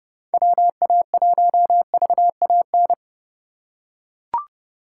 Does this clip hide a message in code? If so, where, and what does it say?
Yes: Morse code from 0.3 to 2.9 s, heard as a keyed tone, about 31 wpm, 700 Hz. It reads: WA1VAN